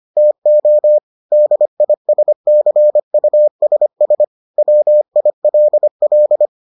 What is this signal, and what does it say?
Morse code, heard as a keyed tone, about 25 wpm, 600 Hz. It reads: TO DISCUSS WILL